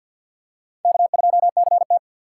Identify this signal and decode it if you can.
Morse code, heard as a keyed tone, about 49 wpm, 700 Hz. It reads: K2CT
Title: USA Call Signs - 4x Character Spacing - 50wpm